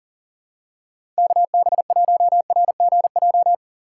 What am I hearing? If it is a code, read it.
Morse code, heard as a keyed tone, about 39 wpm, 700 Hz. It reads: KB1RGJ
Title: USA Call Signs - 3x Character Spacing - 40wpm